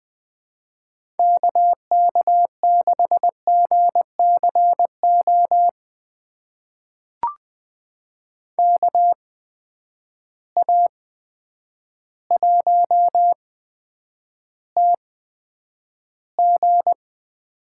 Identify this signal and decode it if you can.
Morse code, heard as a keyed tone, about 20 wpm, spaced as for 9 wpm, 700 Hz. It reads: KK6GCO KA1TG
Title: USA Call Signs - 8x Character Spacing - 20wpm